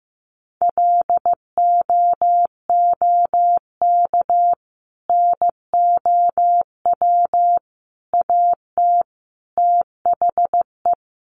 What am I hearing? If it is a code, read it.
Morse code, heard as a keyed tone, about 15 wpm, 700 Hz. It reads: LOOK NOW AT THE
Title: Sentences from Top 1400 Words 15wpm